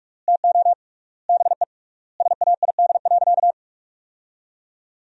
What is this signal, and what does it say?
Morse code, heard as a keyed tone, about 46 wpm, 700 Hz. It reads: TO BE SAID.
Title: 3x Word-Spacing - Sentences from Top 500 Words - 45wpm